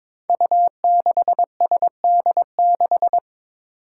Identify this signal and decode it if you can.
Morse code, heard as a keyed tone, about 22 wpm, 700 Hz. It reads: U6SD6
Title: Five Challenging Letter-Number Combinations 22wpm